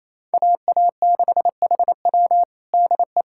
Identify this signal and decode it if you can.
Morse code, heard as a keyed tone, about 28 wpm, 700 Hz. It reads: AA6HW DE
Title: USA Callsign DE Callsign 28wpm